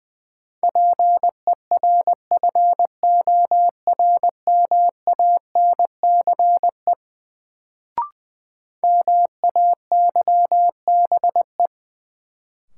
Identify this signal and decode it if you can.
Morse code, heard as a keyed tone, about 20 wpm, 700 Hz. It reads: PERFORMANCE MAYBE
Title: Sentences from Top 1000 Words - New Word Drill - 20wpm